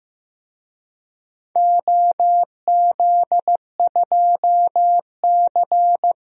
Morse code, heard as a keyed tone, about 15 wpm, 700 Hz. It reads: OZ2C
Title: DXCC-100 Call Signs - 4 Characters - 15wpm